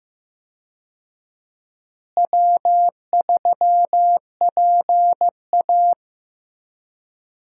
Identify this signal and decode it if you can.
Morse code, heard as a keyed tone, about 16 wpm, 700 Hz. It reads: W3PA
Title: USA Call Signs - 4 Characters - Repeated 3x - 15wpm